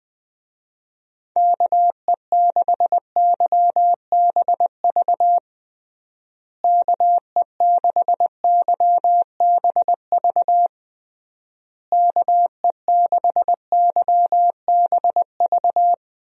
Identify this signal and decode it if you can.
Morse code, heard as a keyed tone, about 21 wpm, 700 Hz. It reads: KE6YBV KE6YBV KE6YBV
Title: USA Call Signs - 6 Characters - Repeated 3x - 20wpm